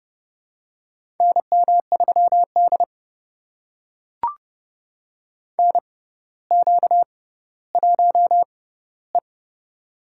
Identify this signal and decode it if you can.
Morse code, heard as a keyed tone, about 30 wpm, spaced as for 15 wpm, 700 Hz. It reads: NM3D NQ1E
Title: USA Call Signs - 4 Characters - 6x Character Spacing - 30wpm